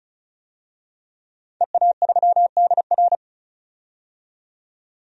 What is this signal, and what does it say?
Morse code, heard as a keyed tone, about 34 wpm, 700 Hz. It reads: EA3DR
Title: DXCC-100 Call Signs - 5 Characters - 35wpm